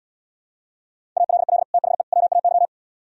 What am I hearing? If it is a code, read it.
Morse code, heard as a keyed tone, about 77 wpm, 700 Hz. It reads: ALL THE WAY